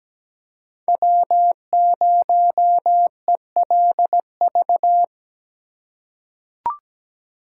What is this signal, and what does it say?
Morse code, heard as a keyed tone, about 17 wpm, 700 Hz. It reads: W0ELV
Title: Code Groups - 5 Letter:Numbers - 17wpm